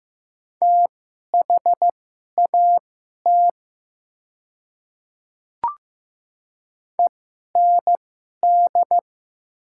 Morse code, heard as a keyed tone, about 15 wpm, spaced as for 11 wpm, 700 Hz. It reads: THAT END
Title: 2x Character Spacing - Top 200 Words - 15wpm